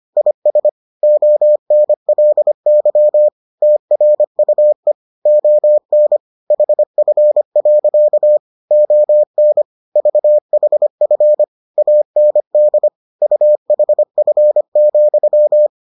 Morse code, heard as a keyed tone, about 25 wpm, 600 Hz. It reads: IS ONLY TRUE ON HF. ON VHF AND UHF,